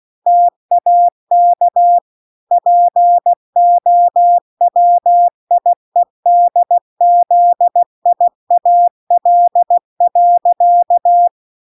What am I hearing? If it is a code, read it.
Morse code, heard as a keyed tone, about 16 wpm, 700 Hz. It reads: TAK POWIEDZIAL.